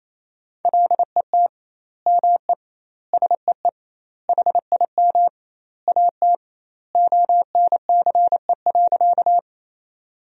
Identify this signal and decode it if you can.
Morse code, heard as a keyed tone, about 28 wpm, 700 Hz. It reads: LET ME SEE HIM AT ONCE.
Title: 2x Word-Spacing - Sentences from Top 300 Words - 28wpm